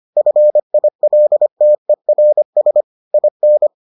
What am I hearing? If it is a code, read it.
Morse code, heard as a keyed tone, about 25 wpm, 600 Hz. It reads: FILTERS IN